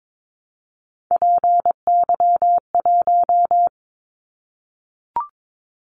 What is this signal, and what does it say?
Morse code, heard as a keyed tone, about 22 wpm, 700 Hz. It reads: PY1
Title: Callsign Prefixes for Top 100 DXCC - 22wpm